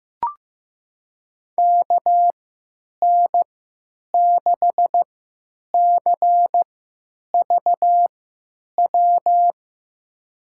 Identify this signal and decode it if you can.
Morse code, heard as a keyed tone, about 15 wpm, spaced as for 9 wpm, 700 Hz. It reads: KN6CVW
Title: USA Call Signs - 3x Character Spacing - 15wpm